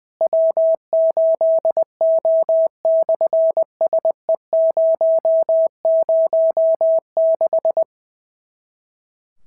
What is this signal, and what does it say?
Morse code, heard as a keyed tone, about 20 wpm, 650 Hz. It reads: W8O/SE006
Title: SOTA QSO Practice 20wpm